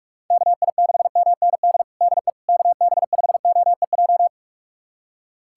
Morse code, heard as a keyed tone, about 46 wpm, 700 Hz. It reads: KI6MND DE KB5OEJ
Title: USA Callsign DE Callsign 45wpm